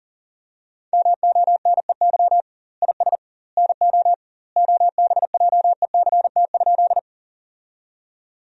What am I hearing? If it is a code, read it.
Morse code, heard as a keyed tone, about 37 wpm, 700 Hz. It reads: MONEY IS NO OBJECT?
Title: 2x Word-Spacing - Sentences from Top 500 Words - 40wpm